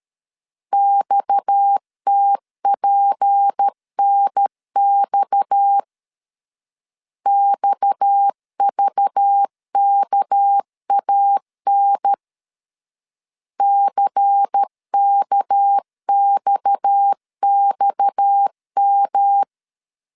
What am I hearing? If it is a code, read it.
Morse code, heard as a keyed tone, about 13 wpm, 800 Hz. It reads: XTPNX XVKAN CKXXM
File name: LEZIONI / 6 PKX0 / 6 PKX 12wpm 2024